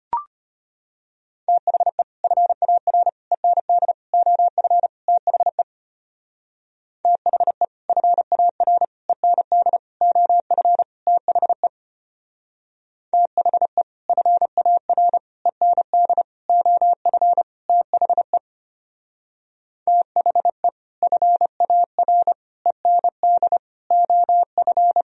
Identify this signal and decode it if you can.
Morse code, heard as a keyed tone, about 32 wpm, 700 Hz. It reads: THE FAR END OF THE THE FAR END OF THE THE FAR END OF THE THE FAR END OF